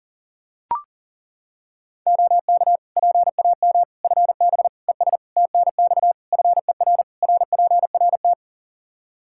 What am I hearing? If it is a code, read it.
Morse code, heard as a keyed tone, about 40 wpm, 700 Hz. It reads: OK PAM FB ES TNX FER RPRT
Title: K1USN SST QSO Practice 40wpm